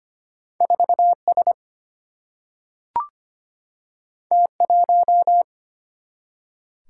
Morse code, heard as a keyed tone, about 25 wpm, 700 Hz. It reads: 4S T1
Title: Single-Letter Single-Number 25wpm